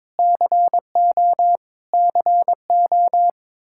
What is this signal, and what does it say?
Morse code, heard as a keyed tone, about 22 wpm, 700 Hz. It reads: CO CO
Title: Practice QSO - Exchange 1 - Beginning - 22wpm